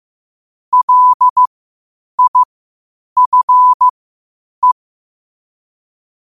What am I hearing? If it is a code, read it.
Morse code, heard as a keyed tone, about 15 wpm, spaced as for 8 wpm, 1000 Hz. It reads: LIFE
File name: qotdfiles / 8wpm